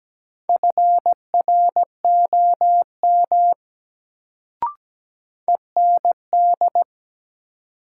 Morse code, heard as a keyed tone, about 17 wpm, 700 Hz. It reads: FROM END
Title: Sets of 2 Words - New Word Drill - 17wpm